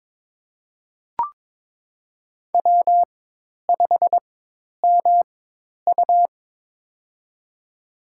Morse code, heard as a keyed tone, about 22 wpm, spaced as for 10 wpm, 700 Hz. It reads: W5MU
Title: USA Call Signs - 4x Character Spacing - 22wpm